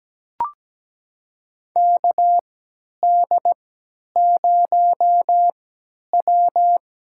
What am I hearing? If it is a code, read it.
Morse code, heard as a keyed tone, about 17 wpm, spaced as for 10 wpm, 700 Hz. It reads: KD0W
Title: USA Call Signs - 3x Character Spacing - 17wpm